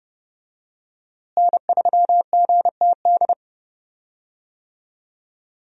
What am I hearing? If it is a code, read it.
Morse code, heard as a keyed tone, about 30 wpm, 700 Hz. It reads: N3GTD
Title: USA Call Signs - 5 Characters - Repeated 2x - 30wpm